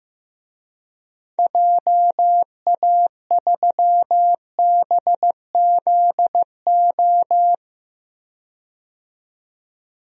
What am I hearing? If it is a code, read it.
Morse code, heard as a keyed tone, about 15 wpm, 700 Hz. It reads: JA3BZO
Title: DXCC-100 Call Signs - 6 Characters - 15wpm